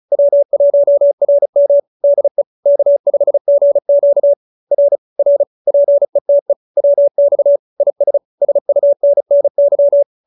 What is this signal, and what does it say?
Morse code, heard as a keyed tone, about 35 wpm, 590 Hz. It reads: W1RM DE K5GQ R R PETE WX IS SUNNY